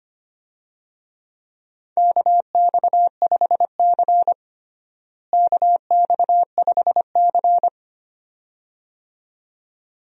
Morse code, heard as a keyed tone, about 25 wpm, 700 Hz. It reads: KX5C KX5C
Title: USA Call Signs - 4 Characters - Repeated 2x - 25wpm